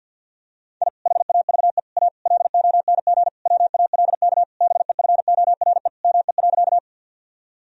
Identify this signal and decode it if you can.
Morse code, heard as a keyed tone, about 47 wpm, 700 Hz. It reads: I HAVE A LONG WALK BEFORE ME.
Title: Sentences from Top 1000 Words - Review Part 1 - 50wpm